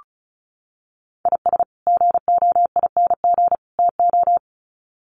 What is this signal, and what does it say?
Morse code, heard as a keyed tone, about 34 wpm, 700 Hz. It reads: IS GOING TO
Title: Sets of 3 Words 35wpm